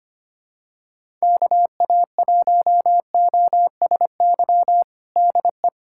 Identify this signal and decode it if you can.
Morse code, heard as a keyed tone, about 25 wpm, 700 Hz. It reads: KA1OSY DE
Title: USA Callsign DE Callsign - 6 Characters - 25wpm